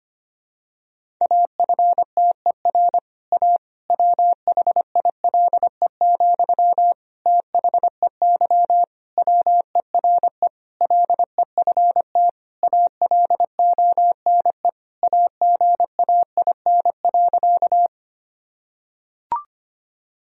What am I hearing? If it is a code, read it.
Morse code, heard as a keyed tone, about 25 wpm, 700 Hz. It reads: AFTER A WHILE, THEY WERE LEFT ALONE AGAIN.
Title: Sentences from Top 1000 Words - Review Part 1 - 25wpm